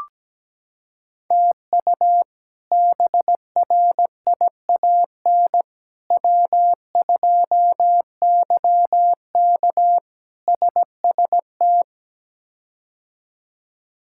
Morse code, heard as a keyed tone, about 17 wpm, 700 Hz. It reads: TU BRIAN W2YK SST
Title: K1USN SST QSO Practice - 17wpm